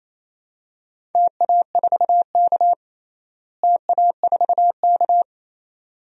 Morse code, heard as a keyed tone, about 28 wpm, 700 Hz. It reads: TA4K TA4K
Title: DXCC-100 Call Signs - 4 Characters - Repeated 2x - 28wpm